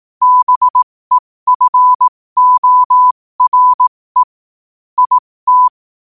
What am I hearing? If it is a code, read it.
Morse code, heard as a keyed tone, about 18 wpm, spaced as for 15 wpm, 1000 Hz. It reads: BEFORE IT
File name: qotdfiles / 15wpm